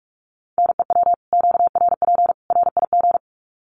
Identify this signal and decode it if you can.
Morse code, heard as a keyed tone, about 44 wpm, 700 Hz. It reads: NEW QRP RIG
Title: Common QSO Phrases 45wpm